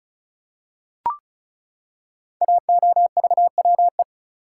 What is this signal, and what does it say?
Morse code, heard as a keyed tone, about 36 wpm, 700 Hz. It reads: AOVWE
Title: Code Groups - 5 Letters - 35wpm